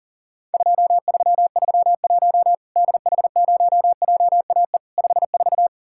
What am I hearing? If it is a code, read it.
Morse code, heard as a keyed tone, about 40 wpm, 700 Hz. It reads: 2331 DH0JAE 54
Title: Worked All Europe DX Contest - QTC Exchange - 40wpm